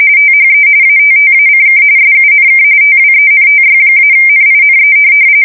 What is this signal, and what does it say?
RTTY, about 45.45 baud. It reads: THE ATTACK SUNK TWENTY-NINE
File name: fsk170 / USS-New-Jersey